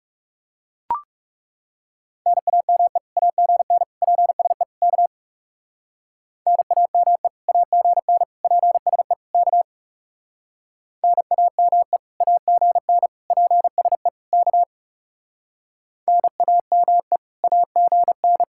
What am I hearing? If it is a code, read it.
Morse code, heard as a keyed tone, about 40 wpm, 700 Hz. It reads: NAME AGN PSE K NAME AGN PSE K NAME AGN PSE K NAME AGN